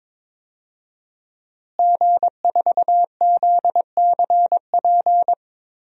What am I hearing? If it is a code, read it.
Morse code, heard as a keyed tone, about 22 wpm, 700 Hz. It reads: G4ZCP
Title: DXCC-100 Call Signs - 5 Characters - Repeated 2x - 22wpm